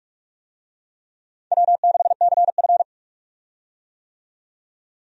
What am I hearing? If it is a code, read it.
Morse code, heard as a keyed tone, about 46 wpm, 700 Hz. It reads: W6CF
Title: USA Call Signs - 4 Characters - 45wpm